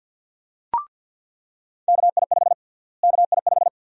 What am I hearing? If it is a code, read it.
Morse code, heard as a keyed tone, about 49 wpm, 700 Hz. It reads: KI5 KI5
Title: USA Call Sign Prefixes - Repeated 2x - 50wpm